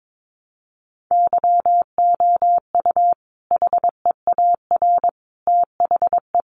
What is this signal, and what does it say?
Morse code, heard as a keyed tone, about 22 wpm, 700 Hz. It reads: YOU HEAR THE
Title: Sentences from Top 900 Words 22wpm